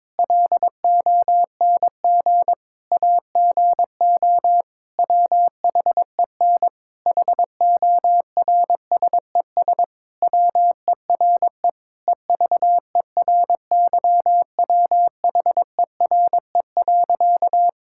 Morse code, heard as a keyed tone, about 22 wpm, 700 Hz. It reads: LONG AGO WHEN HORSES WERE EVERYWHERE.
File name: practice / 22wpm / Sentences with 8-Word Phrases - 22wpm